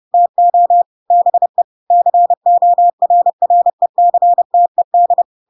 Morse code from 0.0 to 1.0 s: TO